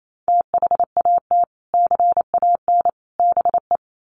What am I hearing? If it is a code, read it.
Morse code, heard as a keyed tone, about 28 wpm, 700 Hz. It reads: THAT CAN BE
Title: Sets of 3 Words 28wpm